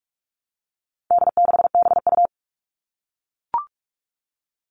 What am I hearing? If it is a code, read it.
Morse code, heard as a keyed tone, about 44 wpm, 700 Hz. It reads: D6BU
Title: All Cut Numbers - 4-Digit 45wpm